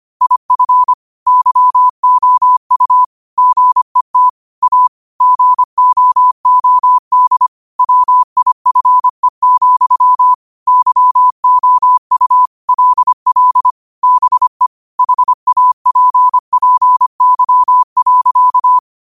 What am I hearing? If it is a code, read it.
Morse code, heard as a keyed tone, about 25 wpm, 1000 Hz. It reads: IF YOU GET A GOOD WIFE, YOU LL BE HAPPY.
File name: 25wpm